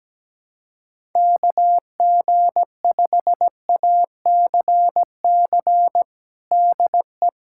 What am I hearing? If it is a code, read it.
Morse code, heard as a keyed tone, about 17 wpm, 700 Hz. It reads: KG5ACC DE
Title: USA Callsign DE Callsign 17wpm